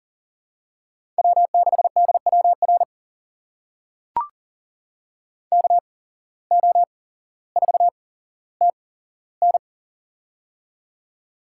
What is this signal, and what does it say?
Morse code, heard as a keyed tone, about 40 wpm, spaced as for 17 wpm, 700 Hz. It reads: W6DWR KO4TN
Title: USA Call Signs - 5 Characters - 8x Character Spacing - 40wpm